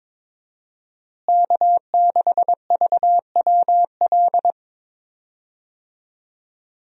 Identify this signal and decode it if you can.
Morse code, heard as a keyed tone, about 22 wpm, 700 Hz. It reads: K6VWL
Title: USA Call Signs - 5 Characters - Repeated 2x - 22wpm